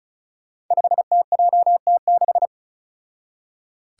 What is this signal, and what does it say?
Morse code, heard as a keyed tone, about 36 wpm, 700 Hz. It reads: 5TJT6
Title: Five Challenging Letter-Number Combinations 35wpm